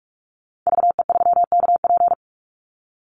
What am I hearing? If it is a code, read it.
Morse code, heard as a keyed tone, about 44 wpm, 700 Hz. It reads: VE3KP
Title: Call Signs - 45wpm